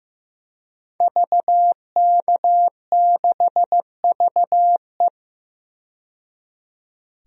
Morse code, heard as a keyed tone, about 15 wpm, 700 Hz. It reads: VK6VE
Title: DXCC-100 Call Signs - 5 Characters - 15wpm